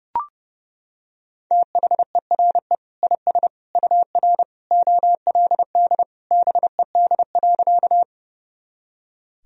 Morse code, heard as a keyed tone, about 30 wpm, 700 Hz. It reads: THERE IS UR OLD BED.
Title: Sentences from Top 500 Words and Abbreviations 30wpm